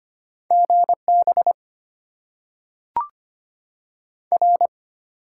Morse code, heard as a keyed tone, about 25 wpm, 700 Hz. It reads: GB R
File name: Top 90 QSO Elements - 8x Character Spacing - 25wpm